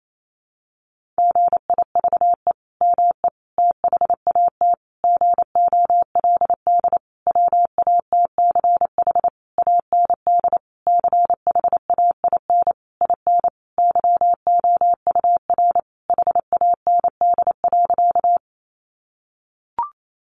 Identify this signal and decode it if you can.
Morse code, heard as a keyed tone, about 28 wpm, 700 Hz. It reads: GIVE ME THAT GOLD WATCH AND CHAIN IN YOUR HAND.
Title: Sentences from Top 1500 Words 28wpm